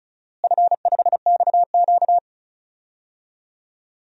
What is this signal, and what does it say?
Morse code, heard as a keyed tone, about 34 wpm, 700 Hz. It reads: F5XQ
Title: DXCC-100 Call Signs - 4 Characters - Repeated 3x - 35wpm